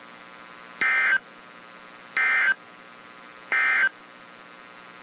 Emergency Alert System (SAME) tones, heard as harsh buzzing bursts. It is heard from 0.7 to 4.0 s.
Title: WZ2523 - NOAA Weather Radio